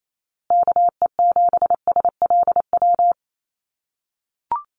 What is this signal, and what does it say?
Morse code, heard as a keyed tone, about 28 wpm, 700 Hz. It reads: KE7SLW
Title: USA Call Signs Repeated 2x 28wpm